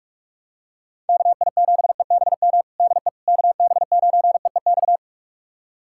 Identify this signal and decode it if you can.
Morse code, heard as a keyed tone, about 46 wpm, 700 Hz. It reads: KI7EBM DE KB9EEX
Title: USA Callsign DE Callsign - 6 Characters - 45wpm